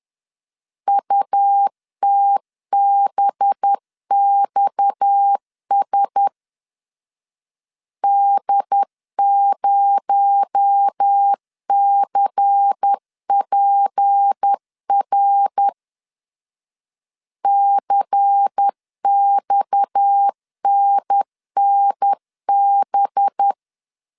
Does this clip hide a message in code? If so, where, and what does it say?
Yes: Morse code from 0.9 to 23.5 s, heard as a keyed tone, about 10 wpm, 800 Hz. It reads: UTBXS D0CPR CXNNB